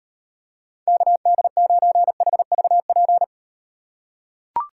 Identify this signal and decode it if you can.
Morse code, heard as a keyed tone, about 38 wpm, 700 Hz. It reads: KD9HVP